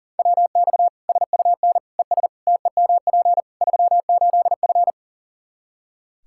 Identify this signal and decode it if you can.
Morse code, heard as a keyed tone, about 40 wpm, 700 Hz. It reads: WX SUN ES TEMP 38F
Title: Practice QSO - Exchange 2 - Info - 40wpm